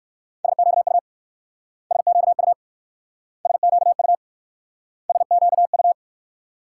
Morse code, heard as a keyed tone, about 61 wpm, 700 Hz. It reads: SQU SQU SQU SQU